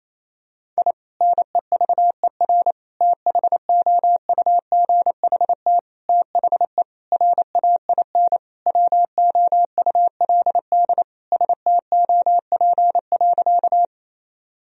Morse code, heard as a keyed tone, about 28 wpm, 700 Hz. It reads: I NEVER THOUGHT THE RAIN WOULD STOP.
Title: Sentences from Top 1000 Words - Review Part 1 - 28wpm